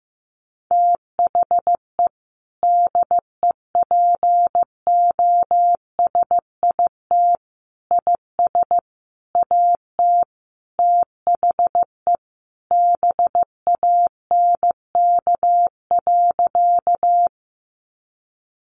Morse code, heard as a keyed tone, about 15 wpm, 700 Hz. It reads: THE DEPOSIT IS AT THE BANK.
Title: Sentences from Top 1300 Words 15wpm